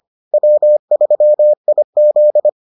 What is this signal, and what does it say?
Morse code, heard as a keyed tone, about 25 wpm, 600 Hz. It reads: W3IZ